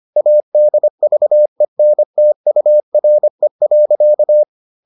Morse code, heard as a keyed tone, about 25 wpm, 600 Hz. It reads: ADVENTURE.